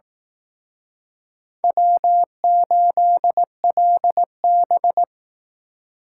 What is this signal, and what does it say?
Morse code, heard as a keyed tone, about 19 wpm, 700 Hz. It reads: W8LB